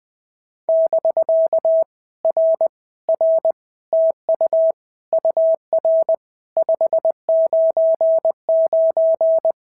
Morse code, heard as a keyed tone, about 20 wpm, 650 Hz. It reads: <BK> R R TU UR 599
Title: POTA QSO Practice - 20wpm